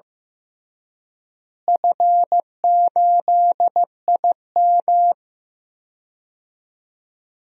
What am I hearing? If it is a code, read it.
Morse code, heard as a keyed tone, about 16 wpm, 700 Hz. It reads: F8IM
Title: DXCC-100 Call Signs - 4 Characters - Repeated 3x - 15wpm